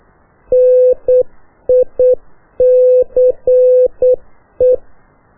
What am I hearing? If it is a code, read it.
Morse code, heard as a keyed tone, about 8 wpm, 500 Hz. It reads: NICE
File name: qso1-qrs